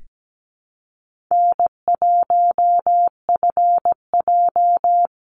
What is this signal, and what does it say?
Morse code, heard as a keyed tone, about 17 wpm, 700 Hz. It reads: N1FJ
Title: NA SOTA Activator Callsigns 17wpm